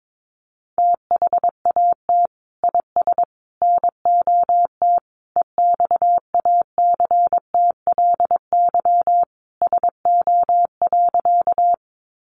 Morse code, heard as a keyed tone, about 22 wpm, 700 Hz. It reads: THAT IS NOT EXACTLY SO.
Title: Sentences from Top 900 Words 22wpm